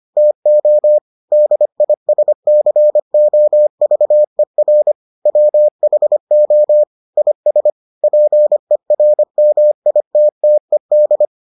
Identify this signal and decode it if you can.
Morse code, heard as a keyed tone, about 25 wpm, 600 Hz. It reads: TO DISCOVER WHO IS PERMITTED